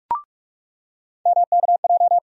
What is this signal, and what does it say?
Morse code, heard as a keyed tone, about 43 wpm, 700 Hz. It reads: MKJ
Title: USA Call Sign Postfixes - 3 Letters 45wpm